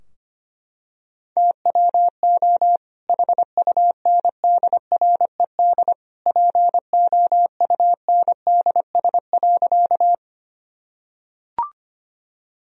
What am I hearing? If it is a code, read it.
Morse code, heard as a keyed tone, about 25 wpm, 700 Hz. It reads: TWO HUNDRED POUNDS.
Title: Sentences from Top 1000 Words - Review Part 2 - 25wpm